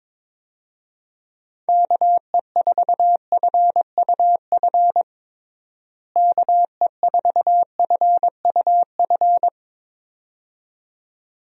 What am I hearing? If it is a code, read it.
Morse code, heard as a keyed tone, about 22 wpm, 700 Hz. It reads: KE4FUF KE4FUF
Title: USA Call Signs - 6 Characters - Repeated 2x - 22wpm